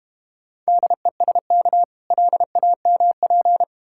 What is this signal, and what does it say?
Morse code, heard as a keyed tone, about 30 wpm, 700 Hz. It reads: DESK LAMP
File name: Easy Words - Speed-Racing with Hints - 25wpm